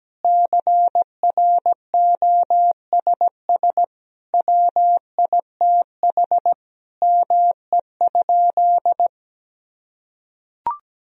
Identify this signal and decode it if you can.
Morse code, heard as a keyed tone, about 17 wpm, 700 Hz. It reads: CROSS WITH ME?
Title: Sentences from Top 1000 Words - Review Part 1 - 17wpm